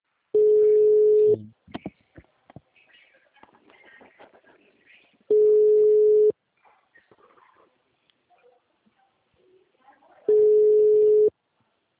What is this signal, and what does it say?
Ringback tone (Europe, 425 Hz), with speech.
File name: out- -2013- - - .41798